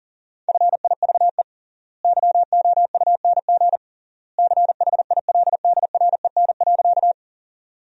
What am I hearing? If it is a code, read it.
Morse code, heard as a keyed tone, about 37 wpm, 700 Hz. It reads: FIVE YOUNG CHILDREN.
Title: 3x Word-Spacing - Sentences from Top 400 Words - 40wpm